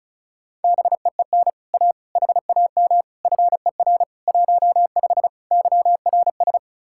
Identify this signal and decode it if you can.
Morse code, heard as a keyed tone, about 45 wpm, 700 Hz. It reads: BEEN A HAM FER 15 YRS